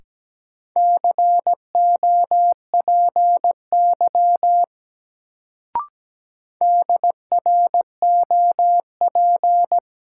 Morse code, heard as a keyed tone, about 17 wpm, 700 Hz. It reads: COPY DROP
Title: Sentences from Top 1400 Words - New Word Drill - 17wpm